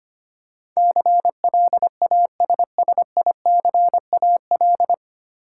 Morse code, heard as a keyed tone, about 25 wpm, 700 Hz. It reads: CLASSICAL
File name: Nine Letter Words - 25wpm